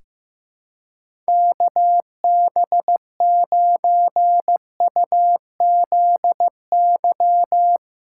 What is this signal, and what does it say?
Morse code, heard as a keyed tone, about 15 wpm, 700 Hz. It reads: KB9UZY
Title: USA Callsign DE Callsign - 6 Characters - 15wpm